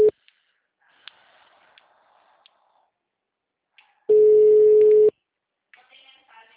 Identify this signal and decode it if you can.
Ringback tone (Europe, 425 Hz), with speech.
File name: out- -2013- - - .14313